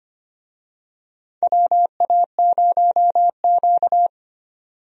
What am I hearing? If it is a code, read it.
Morse code, heard as a keyed tone, about 25 wpm, 700 Hz. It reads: WA0Q
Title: USA Call Signs - 4 Characters - 25wpm